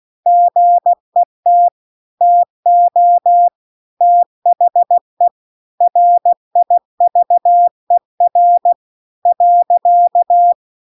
Morse code, heard as a keyed tone, about 16 wpm, 700 Hz. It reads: GET TO THE RIVER .